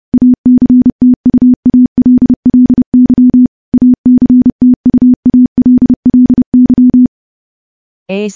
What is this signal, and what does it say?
Morse code, heard as a keyed tone, about 30 wpm, 250 Hz. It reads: ACTUALLY ACTUALLY
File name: morse / morse-training-30WPM